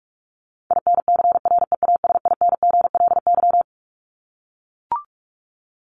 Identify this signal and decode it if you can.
Morse code, heard as a keyed tone, about 44 wpm, 700 Hz. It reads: INCREASINGLY
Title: LY Suffix - Top 100 Words with Base Words 45wpm